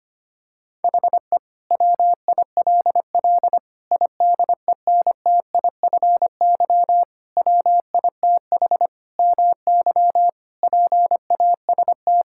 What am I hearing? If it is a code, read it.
Morse code, heard as a keyed tone, about 25 wpm, 700 Hz. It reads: HE WILL IDENTIFY WITH MY PAST
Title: Sentences from Top 1000 Words - Review Part 6 - 25wpm